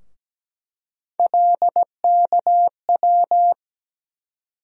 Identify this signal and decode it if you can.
Morse code, heard as a keyed tone, about 17 wpm, 700 Hz. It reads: LKW
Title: USA Call Sign Postfixes - 3 Letters 17wpm